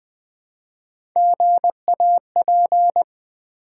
Morse code, heard as a keyed tone, about 20 wpm, 700 Hz. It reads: GAP
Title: Build A Word - 1 Letter At A Time - 3-Letter Words - 20wpm